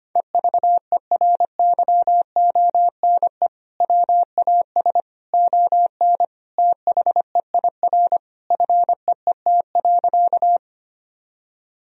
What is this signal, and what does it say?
Morse code, heard as a keyed tone, about 25 wpm, 700 Hz. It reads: EVERYONE WAS ON THEIR FEET.
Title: Sentences from Top 1000 Words - Review Part 1 - 25wpm